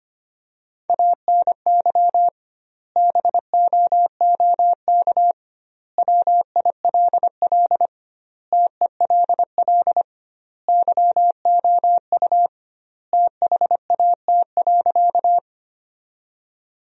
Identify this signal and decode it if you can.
Morse code, heard as a keyed tone, about 25 wpm, 700 Hz. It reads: ANY BOOK WILL TELL YOU THAT.
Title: 2x Word-Spacing - Sentences from Top 300 Words - 25wpm